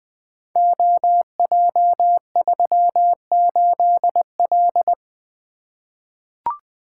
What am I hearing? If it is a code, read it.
Morse code, heard as a keyed tone, about 20 wpm, 700 Hz. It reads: OJ38L
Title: Code Groups - 5 Letter:Numbers - 20wpm